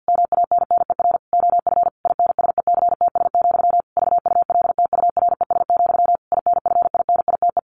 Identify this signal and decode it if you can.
Morse code, heard as a keyed tone, about 50 wpm, 700 Hz. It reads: MANNER OF INSECTS, VULTURES, INFINITE
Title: A Sound Of Thunder 50wpm